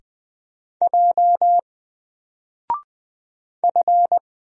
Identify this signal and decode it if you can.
Morse code, heard as a keyed tone, about 20 wpm, 700 Hz. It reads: J F
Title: Single Character (Letter:Number:Punctuation:Prosign) 20wpm